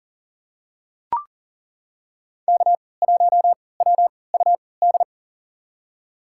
Morse code, heard as a keyed tone, about 40 wpm, spaced as for 23 wpm, 700 Hz. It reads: K1WUD
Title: USA Call Signs - 5 Characters - 3x Character Spacing - 40wpm